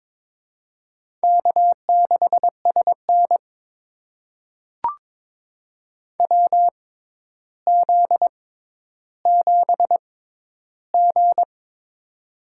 Morse code, heard as a keyed tone, about 22 wpm, spaced as for 11 wpm, 700 Hz. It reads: K6SN WZ7G
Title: USA Call Signs - 4 Characters - 6x Character Spacing - 22wpm